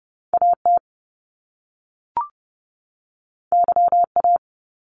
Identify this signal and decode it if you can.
Morse code, heard as a keyed tone, about 30 wpm, 700 Hz. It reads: AT YA
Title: Top Two Letter Combinations in Words 30wpm